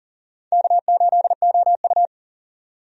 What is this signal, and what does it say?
Morse code, heard as a keyed tone, about 38 wpm, 700 Hz. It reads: K8OU
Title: USA Call Signs - 4 Characters - 40wpm